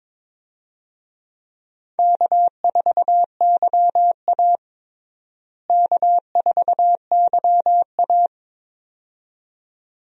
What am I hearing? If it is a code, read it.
Morse code, heard as a keyed tone, about 22 wpm, 700 Hz. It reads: K4YA K4YA
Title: USA Call Signs - 4 Characters - Repeated 2x - 22wpm